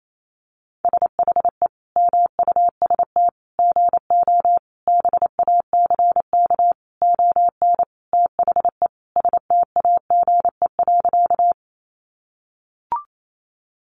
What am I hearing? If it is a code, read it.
Morse code, heard as a keyed tone, about 28 wpm, 700 Hz. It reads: SHE MUST GO BACK ON THE STAGE.
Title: Sentences from Top 1200 Words 28wpm